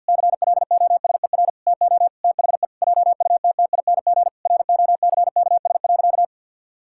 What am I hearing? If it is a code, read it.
Morse code, heard as a keyed tone, about 50 wpm, 700 Hz. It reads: CLOSER TO THE JUTTING ROCKS.
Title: Pangram Challenge v2 - 50wpm